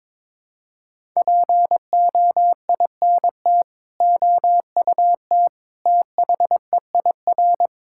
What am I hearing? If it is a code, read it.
Morse code, heard as a keyed tone, about 22 wpm, 700 Hz. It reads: POINT OUT THEIR
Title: Sentences from Top 1000 Words - Review Part 6 - 22wpm